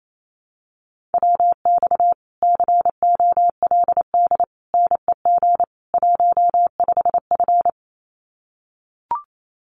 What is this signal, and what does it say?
Morse code, heard as a keyed tone, about 28 wpm, 700 Hz. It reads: WX COLD NEG 15F
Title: Common QSO Phrases Encore 28wpm